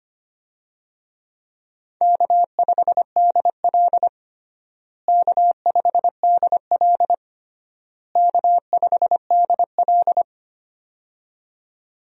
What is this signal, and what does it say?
Morse code, heard as a keyed tone, about 25 wpm, 700 Hz. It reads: K5DL K5DL K5DL
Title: USA Call Signs - 4 Characters - Repeated 3x - 25wpm